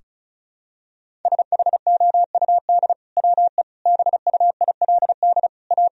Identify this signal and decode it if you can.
Morse code, heard as a keyed tone, about 36 wpm, 700 Hz. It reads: SHOUD WE BUILD A
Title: Sentences from Top 1000 Words - Review Part 4 - 35wpm